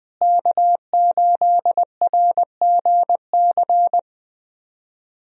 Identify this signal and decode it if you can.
Morse code, heard as a keyed tone, about 21 wpm, 700 Hz. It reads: K8RGC
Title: USA Call Signs - 5 Characters - Repeated 2x - 20wpm